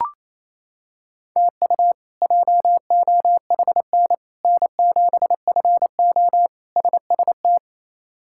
Morse code, heard as a keyed tone, about 27 wpm, 700 Hz. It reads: TU JOHN N7FO SST
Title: K1USN SST QSO Practice - 28wpm